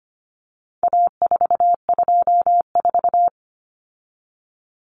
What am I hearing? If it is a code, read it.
Morse code, heard as a keyed tone, about 25 wpm, 700 Hz. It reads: A424
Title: Common Cut Numbers - 4-Digit 25wpm